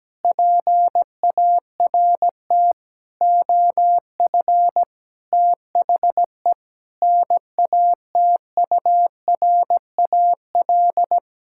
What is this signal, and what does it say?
Morse code, heard as a keyed tone, about 17 wpm, 700 Hz. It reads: PART OF THE NATURAL
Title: Sentences from Top 1000 Words - Review Part 6 - 17wpm